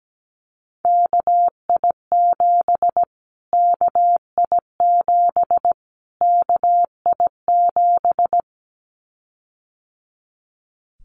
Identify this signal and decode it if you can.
Morse code, heard as a keyed tone, about 17 wpm, 700 Hz. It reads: KI7 KI7 KI7
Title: Callsign Prefixes for Top 25 DXCC - Repeated 3x - 17wpm